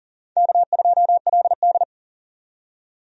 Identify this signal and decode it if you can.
Morse code, heard as a keyed tone, about 40 wpm, 700 Hz. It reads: K2LD
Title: USA Call Signs - 4 Characters - 40wpm